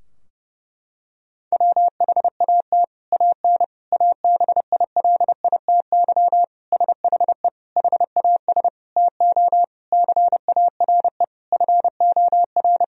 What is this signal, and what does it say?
Morse code, heard as a keyed tone, about 30 wpm, 700 Hz. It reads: WHAT AN ABILITY SHE HAS TO CARE FOR
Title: Sentences from Top 1000 Words - Review Part 3 - 30wpm